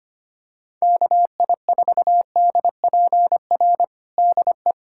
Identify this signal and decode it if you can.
Morse code, heard as a keyed tone, about 25 wpm, 700 Hz. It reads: KI4DPR DE
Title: USA Callsign DE Callsign 25wpm